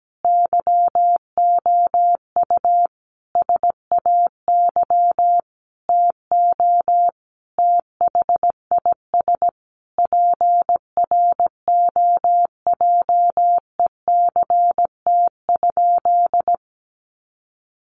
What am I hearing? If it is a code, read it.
Morse code, heard as a keyed tone, about 17 wpm, 700 Hz. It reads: YOU SAY TO THIS PROJECT?
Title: Sentences from Top 1400 Words 17wpm